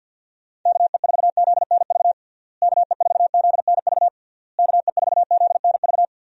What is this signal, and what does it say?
Morse code, heard as a keyed tone, about 48 wpm, 700 Hz. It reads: KE4ZNV KE4ZNV KE4ZNV
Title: USA Call Signs - 6 Characters - Repeated 3x - 50wpm